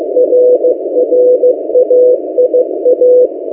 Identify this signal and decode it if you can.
Morse code, heard as a keyed tone, about 15 wpm, 500 Hz. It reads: RRAIA
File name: M08a- -1355utc-8096khz